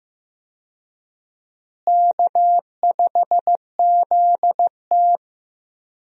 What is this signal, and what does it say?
Morse code, heard as a keyed tone, about 15 wpm, 700 Hz. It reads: K5ZT
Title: USA Call Signs - 4 Characters - Repeated 2x - 15wpm